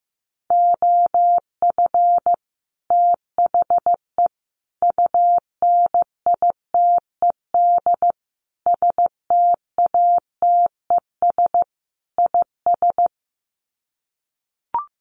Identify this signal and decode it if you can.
Morse code, heard as a keyed tone, about 15 wpm, 700 Hz. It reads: OF THE UNITED STATES IS
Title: Sets of 5 Words Encore2 15wpm